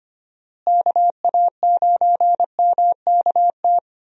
Morse code, heard as a keyed tone, about 25 wpm, 700 Hz. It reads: KA9MKT
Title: USA Call Signs - 6 Characters - Repeated 2x - 25wpm